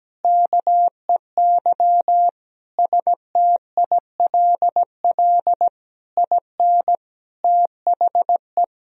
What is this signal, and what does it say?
Morse code, heard as a keyed tone, about 17 wpm, 700 Hz. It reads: KEY STILL IN THE
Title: Sentences from Top 1000 Words - Review Part 4 - 17wpm